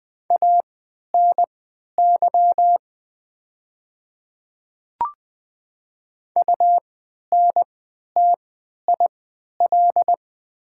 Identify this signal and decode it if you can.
Morse code, heard as a keyed tone, about 20 wpm, spaced as for 11 wpm, 700 Hz. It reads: ANY UNTIL
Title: 3x Character Spacing - Top 300 Words - 20wpm